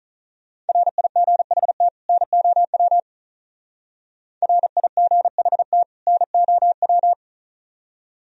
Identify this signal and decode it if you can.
Morse code, heard as a keyed tone, about 45 wpm, 700 Hz. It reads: RIGHT NOW RIGHT NOW